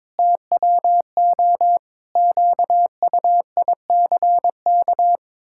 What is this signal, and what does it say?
Morse code, heard as a keyed tone, about 22 wpm, 700 Hz. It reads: TWO QUICK
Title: Pangram Challenge v2 - 22wpm